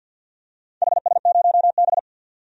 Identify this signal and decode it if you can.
Morse code, heard as a keyed tone, about 52 wpm, 700 Hz. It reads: HS0B
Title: DXCC-100 Call Signs - 4 Characters - 50wpm